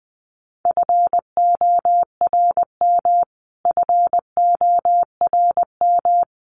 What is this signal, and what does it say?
Morse code, heard as a keyed tone, about 20 wpm, 700 Hz. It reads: FORM FORM
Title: Top 200 Words Repeated 20wpm